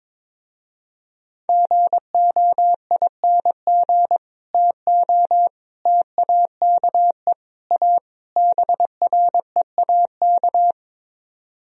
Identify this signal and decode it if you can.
Morse code, heard as a keyed tone, about 22 wpm, 700 Hz. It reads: GOING TO TAKE A BREAK
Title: Speed-Racing - Sets of 5 Words 22wpm